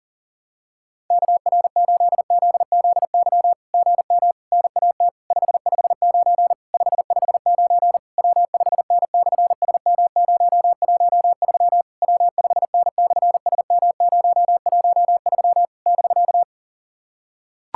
Morse code, heard as a keyed tone, about 40 wpm, 700 Hz. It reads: KR8ZZY GM NAT 559 559 W5N/SM013 W5N/SM013 <BK>
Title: SOTA QSO Practice 40wpm